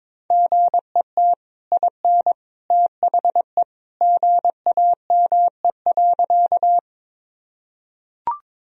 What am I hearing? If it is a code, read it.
Morse code, heard as a keyed tone, about 22 wpm, 700 Hz. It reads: GET IN THE GAME.
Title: Sentences from Top 1000 Words - Review Part 3 - 22wpm